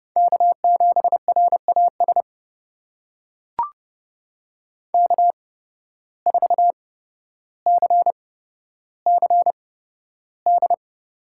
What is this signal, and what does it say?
Morse code, heard as a keyed tone, about 30 wpm, spaced as for 13 wpm, 700 Hz. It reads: K7RAS K4CCD
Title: USA Call Signs - 5 Characters - 8x Character Spacing - 30wpm